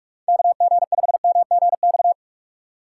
Morse code, heard as a keyed tone, about 46 wpm, 700 Hz. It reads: KG5MGX
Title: USA Call Signs - 6 Characters - 6x Character Spacing - 45wpm